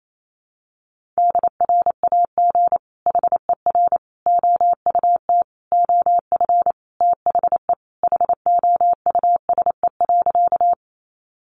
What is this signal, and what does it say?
Morse code, heard as a keyed tone, about 28 wpm, 700 Hz. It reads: DRAG HER OUT OF THE HOUSE.
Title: Sentences from Top 1500 Words 28wpm